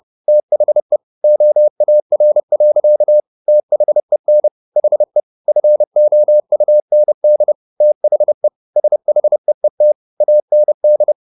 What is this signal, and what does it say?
Morse code, heard as a keyed tone, about 30 wpm, 600 Hz. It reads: THE OAR. THEN HE FOUND THE SHEET AND